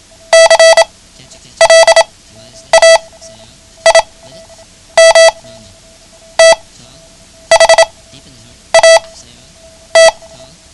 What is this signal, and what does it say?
Morse code, heard as a keyed tone, about 28 wpm, spaced as for 9 wpm, 660 Hz. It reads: CLAIM THAT
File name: ham radio / CodeQuick / Code Quick 09